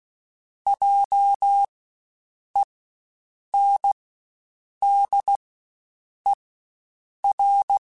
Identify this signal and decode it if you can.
Morse code, heard as a keyed tone, about 16 wpm, spaced as for 7 wpm, 800 Hz. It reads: JENDER
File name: Spaced Alphas / Spaced Alphas mp3s / Js sp @15